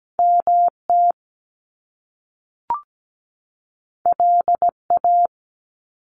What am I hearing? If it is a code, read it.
Morse code, heard as a keyed tone, about 17 wpm, 700 Hz. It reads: MT LA
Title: US States 17wpm